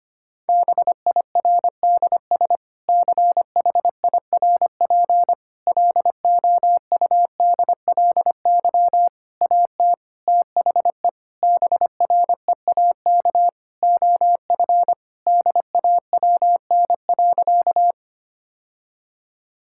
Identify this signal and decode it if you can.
Morse code, heard as a keyed tone, about 25 wpm, 700 Hz. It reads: BIRDS CHIRP LOUDLY AT THE BREAK OF DAWN.
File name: practice / 25wpm / Difficult Sentences with Hints - 25wpm